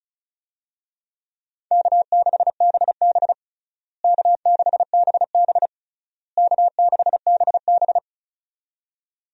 Morse code, heard as a keyed tone, about 34 wpm, 700 Hz. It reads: K6BB K6BB K6BB
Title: USA Call Signs - 4 Characters - Repeated 3x - 35wpm